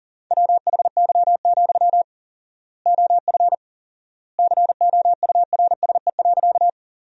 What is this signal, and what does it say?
Morse code, heard as a keyed tone, about 40 wpm, 700 Hz. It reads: WHY, OF COURSE.
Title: 4x Word-Spacing - Sentences from Top 500 Words - 40wpm